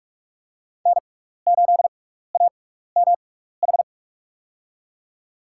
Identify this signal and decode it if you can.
Morse code, heard as a keyed tone, about 44 wpm, spaced as for 15 wpm, 700 Hz. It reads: N8AMH
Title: USA Call Signs - 5 Characters - 6x Character Spacing - 45wpm